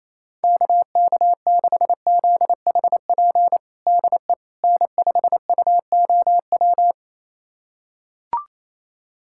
Morse code, heard as a keyed tone, about 28 wpm, 700 Hz. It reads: KK6ZHP DE N5UOW
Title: USA Callsign DE Callsign 28wpm